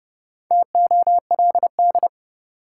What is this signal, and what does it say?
Morse code, heard as a keyed tone, about 30 wpm, 700 Hz. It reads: TOLD
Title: Sentences from Top 400 Words - New Word Drill - 30wpm